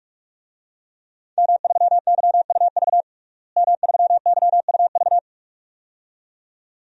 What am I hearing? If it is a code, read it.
Morse code, heard as a keyed tone, about 46 wpm, 700 Hz. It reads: M3YUV M3YUV
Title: DXCC-100 Call Signs - 5 Characters - Repeated 2x - 45wpm